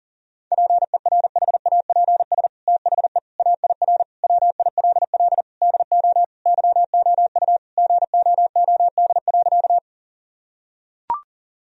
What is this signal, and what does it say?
Morse code, heard as a keyed tone, about 40 wpm, 700 Hz. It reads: PERHAPS THE AIR WILL DO YOU GOOD.
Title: Sentences from Top 1000 Words - Review Part 1 - 40wpm